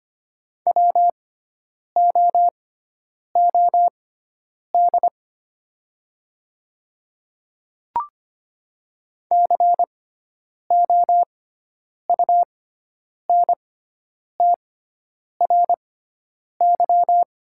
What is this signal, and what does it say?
Morse code, heard as a keyed tone, about 25 wpm, spaced as for 9 wpm, 700 Hz. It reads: WOOD COUNTRY
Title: 6x Character Spacing - Top 300 Words - 25wpm